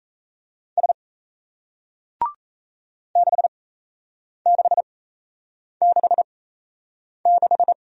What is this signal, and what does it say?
Morse code, heard as a keyed tone, about 38 wpm, spaced as for 8 wpm, 700 Hz. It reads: S 6666